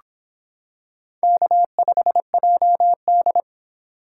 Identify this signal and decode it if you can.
Morse code, heard as a keyed tone, about 25 wpm, 700 Hz. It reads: K5JD